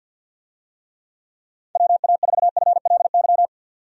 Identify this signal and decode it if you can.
Morse code, heard as a keyed tone, about 52 wpm, 700 Hz. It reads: WA4FLY
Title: USA Call Signs - 6 Characters - 50wpm